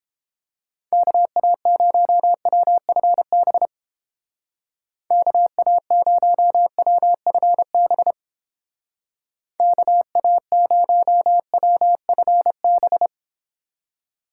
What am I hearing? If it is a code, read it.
Morse code, heard as a keyed tone, about 29 wpm, 700 Hz. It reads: KA0WFB KA0WFB KA0WFB